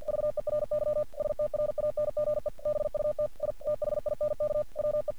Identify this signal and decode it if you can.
Morse code, heard as a keyed tone, about 45 wpm, 600 Hz. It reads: VERY STRANGE BUT I THINK WE
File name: archived / CRSnet